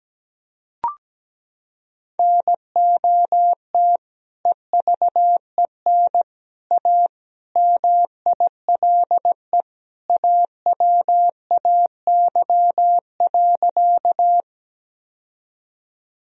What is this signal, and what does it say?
Morse code, heard as a keyed tone, about 17 wpm, 700 Hz. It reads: NOT EVEN A MILE AWAY.
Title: Sentences from Top 1000 Words - Review Part 1 - 17wpm